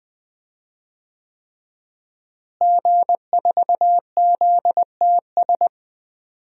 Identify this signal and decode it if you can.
Morse code, heard as a keyed tone, about 20 wpm, 700 Hz. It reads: G4ZTS
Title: DXCC-100 Call Signs - 5 Characters - Repeated 3x - 20wpm